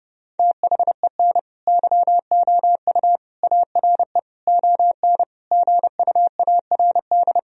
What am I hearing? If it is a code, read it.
Morse code, heard as a keyed tone, about 32 wpm, 700 Hz. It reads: THEN YOU ARE ON GUARD
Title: Sentences from Top 1000 Words - Review Part 4 - 30wpm